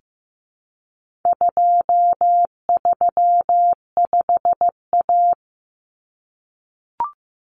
Morse code, heard as a keyed tone, about 15 wpm, 700 Hz. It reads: 235A
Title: Common Cut Numbers - 4-Digit 15wpm